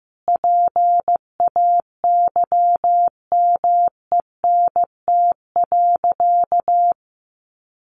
Morse code, heard as a keyed tone, about 15 wpm, 700 Hz. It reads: PAYMENT.
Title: Sentences from Top 1400 Words 15wpm